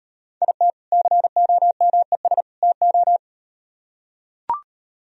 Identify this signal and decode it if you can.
Morse code, heard as a keyed tone, about 38 wpm, 700 Hz. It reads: IT COMES TO